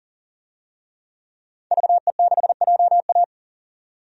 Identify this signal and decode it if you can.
Morse code, heard as a keyed tone, about 39 wpm, 700 Hz. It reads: VE6JA
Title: DXCC-100 Call Signs - 5 Characters - 40wpm